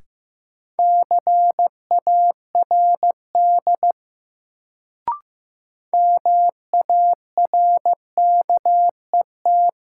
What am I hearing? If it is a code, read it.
Morse code, heard as a keyed tone, about 15 wpm, 700 Hz. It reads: CARD MARKET
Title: Sentences from Top 600 Words - New Word Drill - 15wpm